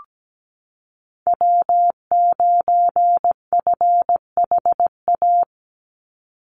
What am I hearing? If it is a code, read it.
Morse code, heard as a keyed tone, about 17 wpm, 700 Hz. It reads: W9FHA
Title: NA SOTA Chaser Callsigns 17wpm